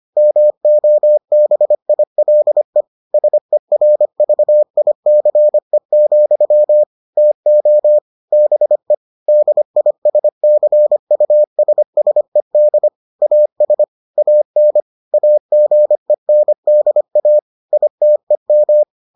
Morse code, heard as a keyed tone, about 25 wpm, 600 Hz. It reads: MOBILE SERVICE, TO BE DISCUSSED AS AN AGENDA ITEM